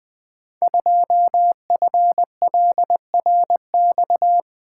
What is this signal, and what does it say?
Morse code, heard as a keyed tone, about 20 wpm, 700 Hz. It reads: 2FLRX
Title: Code Groups - 5 Letter:Numbers - 20wpm